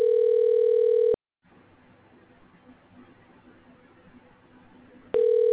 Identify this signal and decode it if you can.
Ringback tone (North America), 440 + 480 Hz.